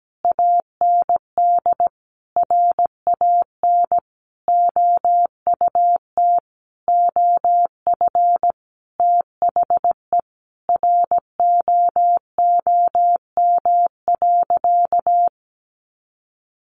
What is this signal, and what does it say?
Morse code, heard as a keyed tone, about 17 wpm, 700 Hz. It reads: AND RAN OUT OF THE ROOM.
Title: Sentences from Top 500 Words 17wpm